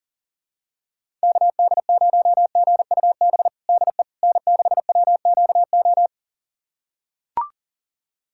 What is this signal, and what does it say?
Morse code, heard as a keyed tone, about 40 wpm, 700 Hz. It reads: KD0GUB DE N6WQO
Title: USA Callsign DE Callsign 40wpm